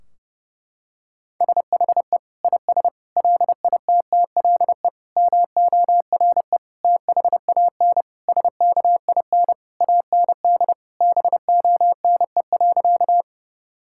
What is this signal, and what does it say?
Morse code, heard as a keyed tone, about 32 wpm, 700 Hz. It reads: SHE IS LITTLE MORE THAN SKIN AND BONE.
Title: Sentences from Top 1600 Words 30wpm